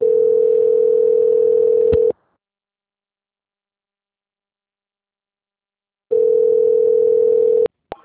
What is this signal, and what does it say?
Ringback tone (North America), 440 + 480 Hz.